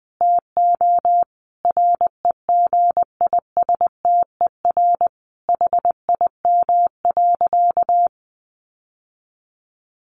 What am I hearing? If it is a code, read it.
Morse code, heard as a keyed tone, about 20 wpm, 700 Hz. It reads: TO REGISTER HIM.
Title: Sentences from Top 1200 Words 20wpm